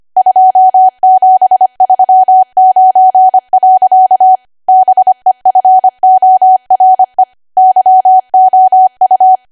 Morse code, heard as a keyed tone, about 25 wpm, 750 Hz. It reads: 2739. BEFORE YOU